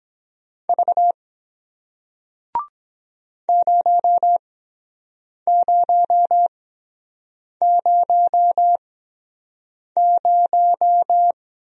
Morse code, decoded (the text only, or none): V 0000